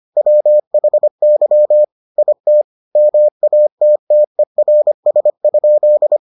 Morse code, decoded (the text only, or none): WHY IT MATTERS?